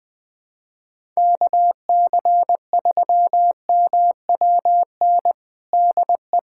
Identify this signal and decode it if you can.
Morse code, heard as a keyed tone, about 21 wpm, 700 Hz. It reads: KC3MWN DE
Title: USA Callsign DE Callsign - 6 Characters - 20wpm